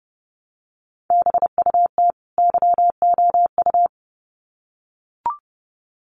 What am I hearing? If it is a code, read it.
Morse code, heard as a keyed tone, about 30 wpm, 700 Hz. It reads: BUT YOU
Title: Sets of 2 Words 30wpm